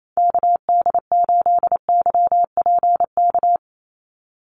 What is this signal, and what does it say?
Morse code, heard as a keyed tone, about 28 wpm, 700 Hz. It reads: KD8YPK
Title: USA Call Signs 28wpm